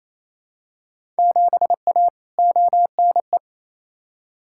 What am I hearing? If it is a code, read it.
Morse code, heard as a keyed tone, about 28 wpm, 700 Hz. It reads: 7A ONE
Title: Field Day Exchanges Encore - 28wpm